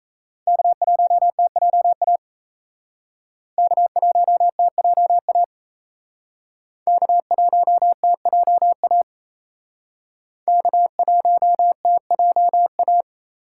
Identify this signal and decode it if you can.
Morse code, heard as a keyed tone, about 36 wpm, 700 Hz. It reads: K1TJA K1TJA K1TJA K1TJA